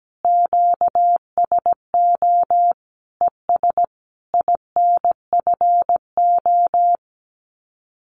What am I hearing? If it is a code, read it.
Morse code, heard as a keyed tone, about 17 wpm, 700 Hz. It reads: QSO ES INFO
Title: Common QSO Phrases Encore 17wpm